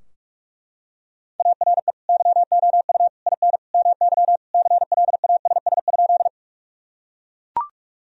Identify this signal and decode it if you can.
Morse code, heard as a keyed tone, about 41 wpm, 700 Hz. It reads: ARE YOU IN MY CLASS?
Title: Sentences from Top 1000 Words - Review Part 2 - 45wpm